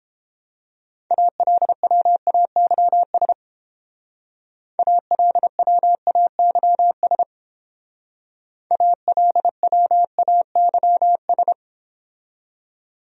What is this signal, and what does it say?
Morse code, heard as a keyed tone, about 30 wpm, 700 Hz. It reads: ALWAYS ALWAYS ALWAYS